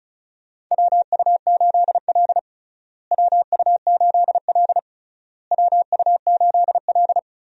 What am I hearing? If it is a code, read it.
Morse code, heard as a keyed tone, about 34 wpm, 700 Hz. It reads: WU8L WU8L WU8L
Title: USA Call Signs - 4 Characters - Repeated 3x - 35wpm